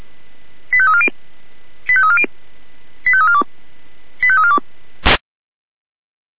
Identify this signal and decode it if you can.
Five-tone selective call (ZVEI), heard as quick runs of five tones, tones of about 70 ms. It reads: 85340 85340 85342 85342